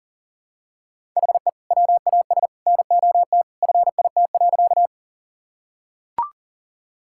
Morse code, heard as a keyed tone, about 39 wpm, 690 Hz. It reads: HE WAS NOT FIT.
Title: Sentences from Top 1000 Words - Review Part 5 - 40wpm